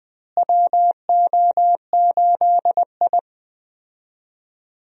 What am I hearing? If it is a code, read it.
Morse code, heard as a keyed tone, about 20 wpm, 700 Hz. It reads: WO8I
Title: USA Call Signs - 4 Characters - Repeated 3x - 20wpm